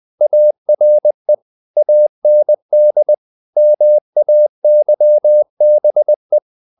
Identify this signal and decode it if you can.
Morse code, heard as a keyed tone, about 20 wpm, 600 Hz. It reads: ARE AND MAYBE
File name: the old man and the sea / the old man and the sea part 1 20wpm